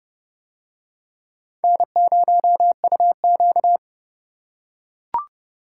Morse code, heard as a keyed tone, about 27 wpm, 690 Hz. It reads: N0UQ